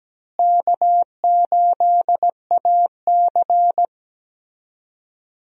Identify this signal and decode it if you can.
Morse code, heard as a keyed tone, about 17 wpm, 700 Hz. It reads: K8AC
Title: USA Call Signs - 4 Characters - Repeated 2x - 17wpm